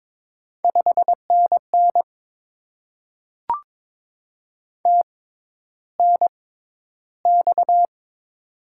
Morse code, heard as a keyed tone, about 22 wpm, spaced as for 11 wpm, 690 Hz. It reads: 5NN TNX
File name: Top 20 QSO Elements - 6x Character Spacing - 22wpm